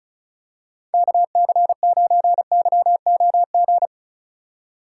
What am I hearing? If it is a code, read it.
Morse code, heard as a keyed tone, about 34 wpm, 700 Hz. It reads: KC9YOG
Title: USA Call Signs - 6 Characters - 6x Character Spacing - 35wpm